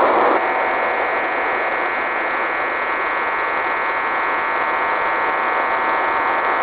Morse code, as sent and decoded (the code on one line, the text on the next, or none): none